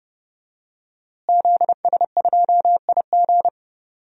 Morse code, heard as a keyed tone, about 28 wpm, 700 Hz. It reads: ZS2IG
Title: DXCC-100 Call Signs - 5 Characters - Repeated 3x - 30wpm